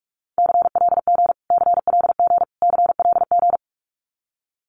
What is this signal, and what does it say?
Morse code, heard as a keyed tone, about 44 wpm, 700 Hz. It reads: CLG CLG CLG
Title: Abbreviations Repeated 3x - 45wpm